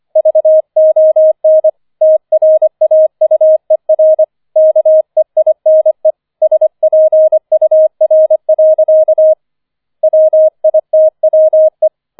Morse code, heard as a keyed tone, about 25 wpm, 630 Hz. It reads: VON TRAUER KEINE SPUR. WITWE